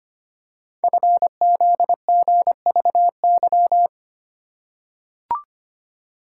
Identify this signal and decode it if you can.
Morse code, heard as a keyed tone, about 25 wpm, 700 Hz. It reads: FZGVY
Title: Code Groups - 5 Letters - 25wpm